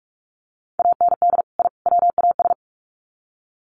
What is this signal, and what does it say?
Morse code, heard as a keyed tone, about 44 wpm, 700 Hz. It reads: AND I WAS
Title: Sets of 3 Words 45wpm